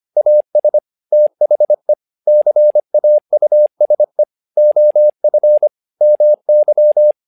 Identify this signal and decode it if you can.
Morse code, heard as a keyed tone, about 25 wpm, 600 Hz. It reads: AS THE CAUSE OF MY
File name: learning center / tower trouble / tower trouble 25wpm